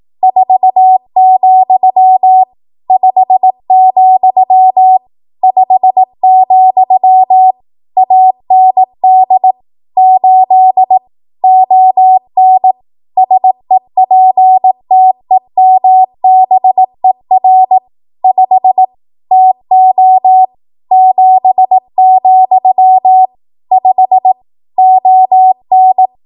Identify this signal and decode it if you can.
Morse code, heard as a keyed tone, about 18 wpm, 750 Hz. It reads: 4, 5, 5, AND 8 ON SEPTEMBER 5 TO 7, 5 ON
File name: Archive / 18 WPM / 18WPM